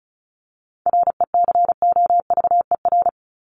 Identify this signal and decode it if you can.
Morse code, heard as a keyed tone, about 34 wpm, 700 Hz. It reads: RECOVER
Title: Top 1751-2000 Words 35wpm